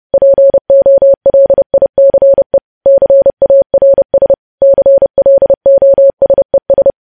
Morse code, heard as a keyed tone, about 30 wpm, 550 Hz. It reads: POLICE CARS CLOSES
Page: news headlines in morse code at 30 wpm